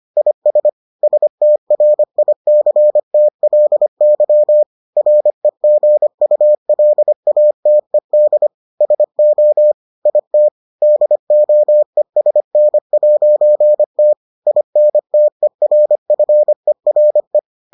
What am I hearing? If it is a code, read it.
Morse code, heard as a keyed tone, about 25 wpm, 600 Hz. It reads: IS STRICTLY REGULATED SO IT DOESN'T INTERFERE